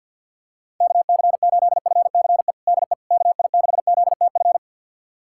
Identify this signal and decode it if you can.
Morse code, heard as a keyed tone, about 47 wpm, 700 Hz. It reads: KC8FCE DE KI6ZTF
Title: USA Callsign DE Callsign - 6 Characters - 50wpm